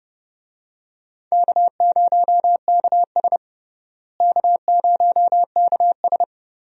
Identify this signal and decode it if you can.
Morse code, heard as a keyed tone, about 30 wpm, 700 Hz. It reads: K0KS K0KS
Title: USA Call Signs - 4 Characters - Repeated 2x - 30wpm